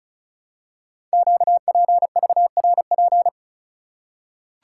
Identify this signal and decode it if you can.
Morse code, heard as a keyed tone, about 36 wpm, 700 Hz. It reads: QPVRP
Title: Code Groups - 5 Letters - 35wpm